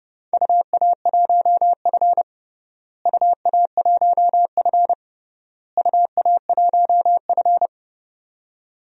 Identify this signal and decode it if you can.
Morse code, heard as a keyed tone, about 30 wpm, 700 Hz. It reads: UA1F UA1F UA1F
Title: DXCC-100 Call Signs - 4 Characters - Repeated 3x - 30wpm